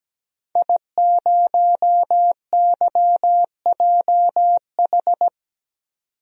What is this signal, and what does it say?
Morse code, heard as a keyed tone, about 17 wpm, 700 Hz. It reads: I0YJH
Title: DXCC-100 Call Signs - 5 Characters - 17wpm